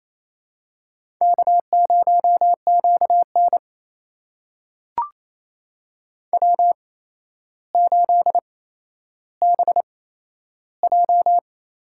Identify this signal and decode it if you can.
Morse code, heard as a keyed tone, about 28 wpm, spaced as for 12 wpm, 700 Hz. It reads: K0QN W8BJ
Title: USA Call Signs - 4 Characters - 8x Character Spacing - 28wpm